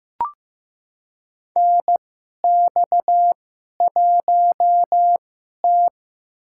Morse code, heard as a keyed tone, about 15 wpm, spaced as for 11 wpm, 700 Hz. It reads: NX1T
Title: USA Call Signs - 2x Character Spacing - 15wpm